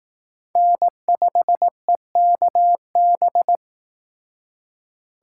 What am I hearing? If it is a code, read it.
Morse code, heard as a keyed tone, about 19 wpm, 700 Hz. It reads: N5EKB